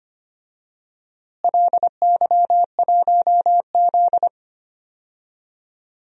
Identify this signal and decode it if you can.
Morse code, heard as a keyed tone, about 25 wpm, 700 Hz. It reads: LY1Z
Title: DXCC-100 Call Signs - 4 Characters - Repeated 3x - 25wpm